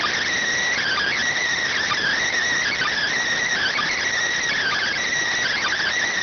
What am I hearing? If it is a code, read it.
Emergency Alert System (SAME) tones (header bursts), heard as harsh buzzing bursts. It is heard from 0.6 to 5.1 s.